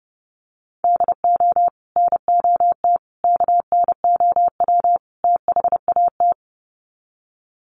Morse code, heard as a keyed tone, about 30 wpm, 700 Hz. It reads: DO NOT KNOW THAT
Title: Sets of 4 Words 30wpm